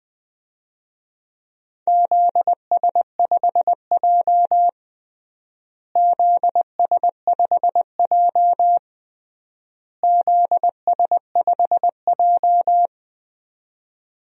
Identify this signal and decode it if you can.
Morse code, heard as a keyed tone, about 20 wpm, 700 Hz. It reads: ZS5J ZS5J ZS5J
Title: DXCC-100 Call Signs - 4 Characters - Repeated 3x - 20wpm